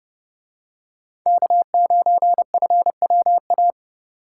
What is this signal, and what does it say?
Morse code, heard as a keyed tone, about 29 wpm, 700 Hz. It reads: K9FWA